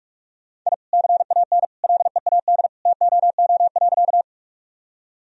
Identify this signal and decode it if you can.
Morse code, heard as a keyed tone, about 46 wpm, 700 Hz. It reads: I CAN LEAD TOO.
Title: Sentences from Top 1000 Words - Review Part 2 - 45wpm